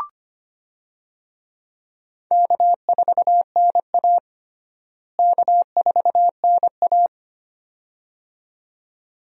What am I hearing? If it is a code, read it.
Morse code, heard as a keyed tone, about 25 wpm, 700 Hz. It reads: K4NA K4NA
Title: USA Call Signs - 4 Characters - Repeated 2x - 25wpm